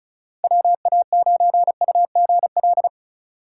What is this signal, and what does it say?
Morse code, heard as a keyed tone, about 34 wpm, 690 Hz. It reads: WA9UGL
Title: USA Call Signs - 4x Character Spacing - 35wpm